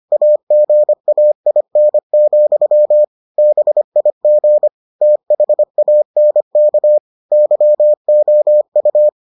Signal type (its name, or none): Morse code